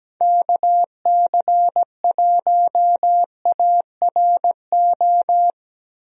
Morse code, heard as a keyed tone, about 17 wpm, 700 Hz. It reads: KC1ARO
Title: USA Call Signs - 6 Characters - Repeated 3x - 17wpm